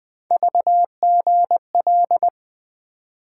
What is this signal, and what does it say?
Morse code, heard as a keyed tone, about 20 wpm, 700 Hz. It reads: VGL
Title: Three Letter-Number Combinations - 20wpm